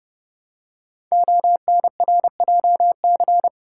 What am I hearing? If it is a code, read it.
Morse code, heard as a keyed tone, about 30 wpm, 700 Hz. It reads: ONRJC
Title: Code Groups - 5 Letters - 30wpm